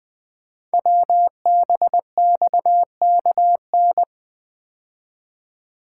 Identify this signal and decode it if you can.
Morse code, heard as a keyed tone, about 20 wpm, 700 Hz. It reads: WBXKN
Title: Code Groups - 5 Letters - 20wpm